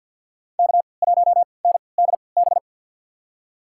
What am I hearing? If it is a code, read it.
Morse code, heard as a keyed tone, about 51 wpm, spaced as for 28 wpm, 690 Hz. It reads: K1NDB
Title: USA Call Signs - 5 Characters - 3x Character Spacing - 50wpm